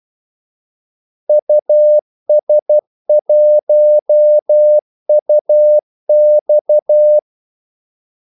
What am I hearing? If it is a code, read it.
Morse code, heard as a keyed tone, about 12 wpm, 600 Hz. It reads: US1UX
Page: 2 ekstra ordmellemrum mellem kaldesignalerne